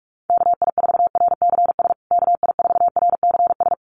Morse code, heard as a keyed tone, about 44 wpm, 700 Hz. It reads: KI4RCS KI4RCS
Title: USA Call Signs Repeated 2x 45wpm